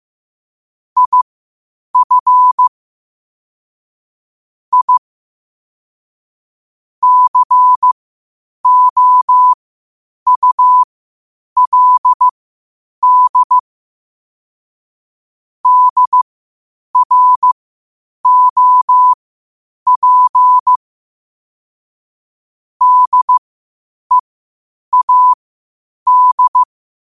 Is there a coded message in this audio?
Yes: Morse code from 1.0 to 26.6 s, heard as a keyed tone, about 15 wpm, spaced as for 8 wpm, 1000 Hz. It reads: IF I COULD DROP DEAD